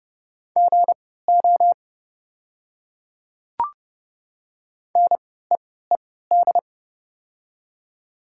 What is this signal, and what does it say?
Morse code, heard as a keyed tone, about 29 wpm, spaced as for 17 wpm, 690 Hz. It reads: GO NEED
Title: 3x Character Spacing - Top 200 Words - 30wpm